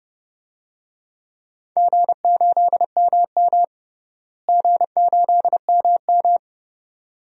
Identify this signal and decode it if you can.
Morse code, heard as a keyed tone, about 30 wpm, 700 Hz. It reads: G8MM G8MM
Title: DXCC-100 Call Signs - 4 Characters - Repeated 2x - 30wpm